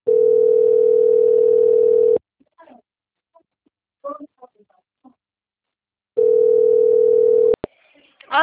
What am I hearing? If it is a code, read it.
Ringback tone (North America), 440 + 480 Hz, with speech.
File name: out- - - - - .61172